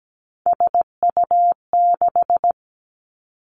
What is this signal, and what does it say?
Morse code, heard as a keyed tone, about 17 wpm, 700 Hz. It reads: SU6
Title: Three Challenging Letter-Number Combinations 17wpm